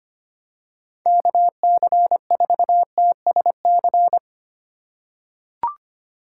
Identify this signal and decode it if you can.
Morse code, heard as a keyed tone, about 25 wpm, 700 Hz. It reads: KC4TSC
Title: USA Call Signs - 6 Characters - 4x Character Spacing - 25wpm